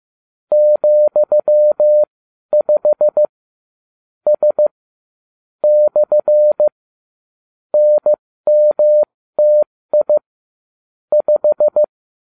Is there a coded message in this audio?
Yes: Morse code from 0.5 to 11.8 s, heard as a keyed tone, about 15 wpm, spaced as for 12 wpm, 600 Hz. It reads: ,5 S / NMTI 5